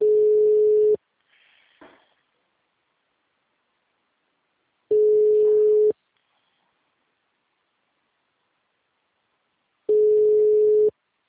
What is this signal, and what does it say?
Ringback tone (Europe, 425 Hz).
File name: out- -2016- - - .11187